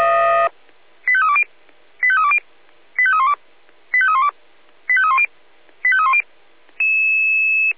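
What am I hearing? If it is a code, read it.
Five-tone selective call (ZVEI), heard as quick runs of five tones, tones of about 70 ms. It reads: 85320 85320 85312 85312 85310 85310
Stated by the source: ZVEI five-tone selective call 85894 received with an SDR.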